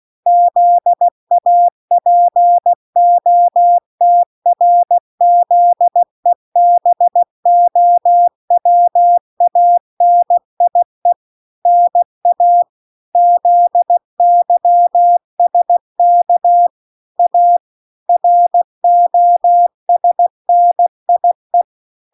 Morse code, heard as a keyed tone, about 16 wpm, 700 Hz. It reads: ZAPOTRZEBOWANIE NA ZYSK A ROSNIE